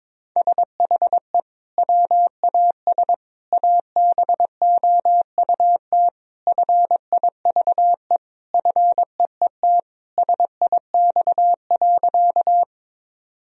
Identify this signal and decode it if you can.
Morse code, heard as a keyed tone, about 22 wpm, 700 Hz. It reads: SHE WAS ABOUT FIVE FEET SIX.
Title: Sentences from Top 1000 Words - Review Part 2 - 22wpm